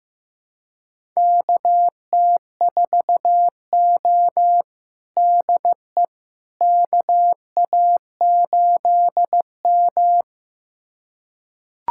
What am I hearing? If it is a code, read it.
Morse code, heard as a keyed tone, about 15 wpm, 700 Hz. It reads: KT4O DE KA8M
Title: USA Callsign DE Callsign - 4 Characters - 15wpm